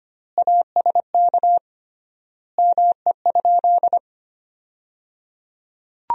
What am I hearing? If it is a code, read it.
Morse code, heard as a keyed tone, about 25 wpm, 700 Hz. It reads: ASK ME?
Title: 3x Word-Spacing - Sentences from Top 200 Words - 25wpm